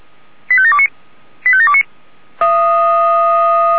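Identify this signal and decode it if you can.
Five-tone selective call (ZVEI), heard as quick runs of five tones, tones of about 70 ms. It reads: 85729 85729
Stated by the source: ZVEI five-tone selective call received on an SDR.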